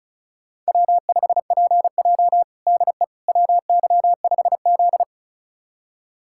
Morse code, heard as a keyed tone, about 36 wpm, 700 Hz. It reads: W5PJ DE WY5Z
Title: USA Callsign DE Callsign - 4 Characters - 35wpm